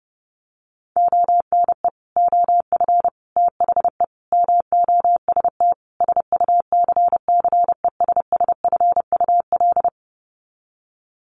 Morse code, heard as a keyed tone, about 30 wpm, 700 Hz. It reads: ONE OF THE MOST SUCCESSFUL
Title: Sets of 5 Words Encore2 30wpm